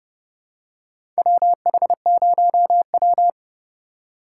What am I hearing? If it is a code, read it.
Morse code, heard as a keyed tone, about 30 wpm, 700 Hz. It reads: WH0W
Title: USA Call Signs - 4x Character Spacing - 30wpm